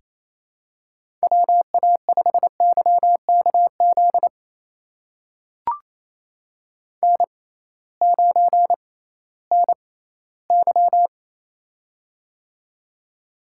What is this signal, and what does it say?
Morse code, heard as a keyed tone, about 29 wpm, spaced as for 17 wpm, 700 Hz. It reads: WA5YKZ N9NY
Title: USA Call Signs - 6x Character Spacing - 28wpm